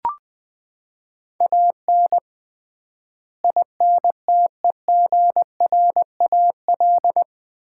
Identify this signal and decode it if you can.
Morse code, heard as a keyed tone, about 20 wpm, 700 Hz. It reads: AN INTEGRAL
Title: 3x Word-Spacing - Sets of 5 Words - 20wpm